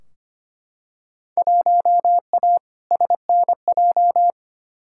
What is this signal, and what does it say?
Morse code, heard as a keyed tone, about 25 wpm, 700 Hz. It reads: 1A SNJ
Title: Field Day Exchanges - 25wpm